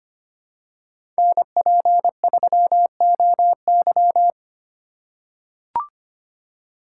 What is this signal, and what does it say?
Morse code, heard as a keyed tone, about 25 wpm, 700 Hz. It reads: NP3OY
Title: USA Call Signs - 5 Characters - 2x Character Spacing - 25wpm